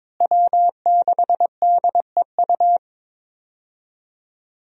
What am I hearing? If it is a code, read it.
Morse code, heard as a keyed tone, about 22 wpm, 700 Hz. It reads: W6DEU
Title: USA Call Signs - 5 Characters - Repeated 3x - 22wpm